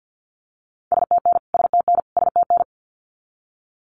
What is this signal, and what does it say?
Morse code, heard as a keyed tone, about 50 wpm, 700 Hz. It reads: STN STN STN
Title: Abbreviations Repeated 3x - 50wpm